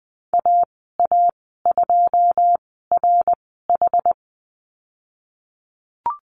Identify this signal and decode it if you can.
Morse code, heard as a keyed tone, about 20 wpm, spaced as for 15 wpm, 700 Hz. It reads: AA2RH
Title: 088 Lesson 15g - Number 5 - Call Signs 20wpm